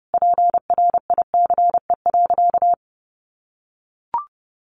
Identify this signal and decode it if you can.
Morse code, heard as a keyed tone, about 30 wpm, 700 Hz. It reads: PRICE.
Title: Sentences from Top 600 Words 30wpm